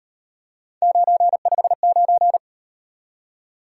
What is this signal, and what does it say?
Morse code, heard as a keyed tone, about 38 wpm, 700 Hz. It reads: 959